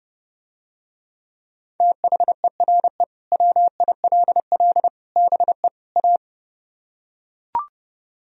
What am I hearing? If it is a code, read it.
Morse code, heard as a keyed tone, about 31 wpm, 700 Hz. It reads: THERE WILL BE A